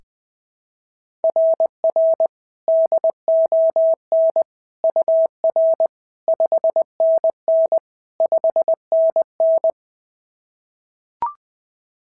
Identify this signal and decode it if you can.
Morse code, heard as a keyed tone, about 21 wpm, 690 Hz. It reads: RR DON UR 5NN 5NN
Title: SOTA QSO Practice 20wpm